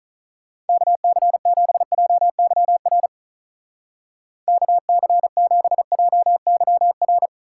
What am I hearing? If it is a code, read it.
Morse code, heard as a keyed tone, about 42 wpm, 700 Hz. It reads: KC7JYR KC7JYR
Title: Speed Racing - USA 2x3 Call Signs 35wpm